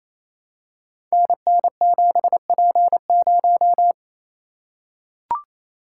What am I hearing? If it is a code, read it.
Morse code, heard as a keyed tone, about 28 wpm, 700 Hz. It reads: NN7P0
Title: Code Groups - 5 Letter:Numbers - 28wpm